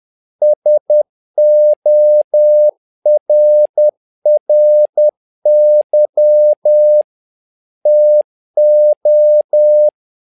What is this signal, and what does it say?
Morse code, heard as a keyed tone, about 10 wpm, 600 Hz. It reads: SORRY TO
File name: learning center / tower trouble / tower trouble 10wpm